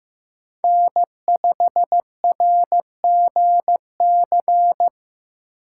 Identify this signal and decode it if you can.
Morse code, heard as a keyed tone, about 15 wpm, 700 Hz. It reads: N5RGC
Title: USA Call Signs - 5 Characters - 3x Character Spacing - 15wpm